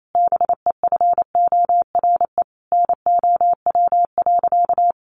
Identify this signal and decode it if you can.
Morse code, heard as a keyed tone, about 28 wpm, 700 Hz. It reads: BEFORE NOW.
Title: Sentences from Top 1300 Words 28wpm